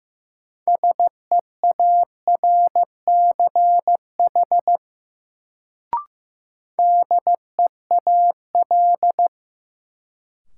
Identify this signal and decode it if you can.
Morse code, heard as a keyed tone, about 16 wpm, 700 Hz. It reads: SEARCH DEAL
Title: Sentences from Top 800 Words - New Word Drill - 15wpm